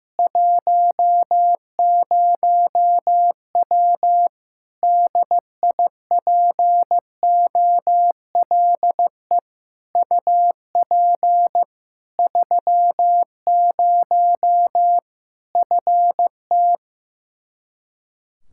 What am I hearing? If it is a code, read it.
Morse code, heard as a keyed tone, about 15 wpm, 700 Hz. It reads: 10W DIPOLE UP 30 FT
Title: Practice QSO - Exchange 2 - Info - 15wpm